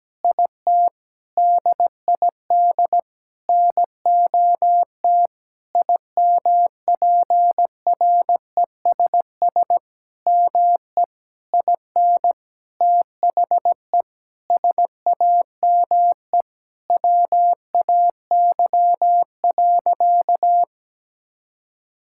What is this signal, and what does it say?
Morse code, heard as a keyed tone, about 17 wpm, 700 Hz. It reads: IT DID NOT IMPRESS ME IN THE SAME WAY.
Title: Sentences from Top 1600 Words 17wpm